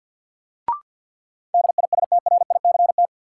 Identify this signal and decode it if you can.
Morse code, heard as a keyed tone, about 51 wpm, 700 Hz. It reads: DISTRICT
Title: Sentences from Top 1600 Words - New Word Drill - 50wpm